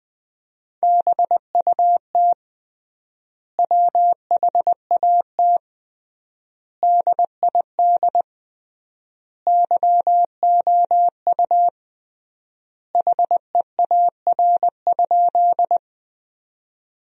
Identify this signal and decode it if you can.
Morse code, heard as a keyed tone, about 20 wpm, 700 Hz. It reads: BUT WHAT DID YOU HEAR?
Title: 3x Word-Spacing - Sentences from Top 300 Words - 20wpm